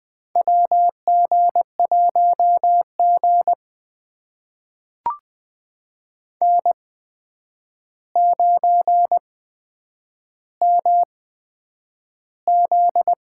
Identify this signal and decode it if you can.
Morse code, heard as a keyed tone, about 20 wpm, spaced as for 9 wpm, 700 Hz. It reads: WG1G N9MZ
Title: USA Call Signs - 4 Characters - 8x Character Spacing - 20wpm